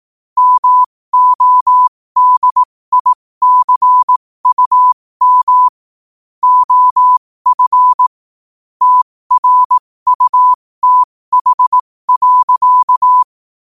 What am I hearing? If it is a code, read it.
Morse code, heard as a keyed tone, about 18 wpm, spaced as for 15 wpm, 1000 Hz. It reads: MODICUM OF TRUTH.
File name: qotdfiles / 15wpm